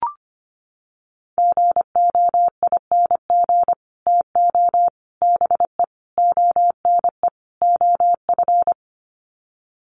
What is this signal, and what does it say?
Morse code, heard as a keyed tone, about 25 wpm, 700 Hz. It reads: GOING TO BE ONE OF
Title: Sets of 5 Words Encore2 25wpm